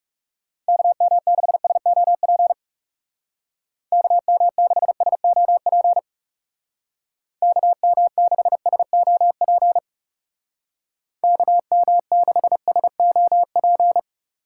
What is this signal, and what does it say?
Morse code, heard as a keyed tone, about 36 wpm, 700 Hz. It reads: KM6SOP KM6SOP KM6SOP KM6SOP
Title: Speed Racing - USA 2x3 Call Signs 30wpm